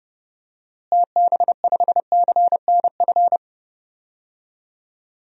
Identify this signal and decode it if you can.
Morse code, heard as a keyed tone, about 30 wpm, 700 Hz. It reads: TB5CNF
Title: DXCC-100 Call Signs - 6 Characters - Repeated 2x - 30wpm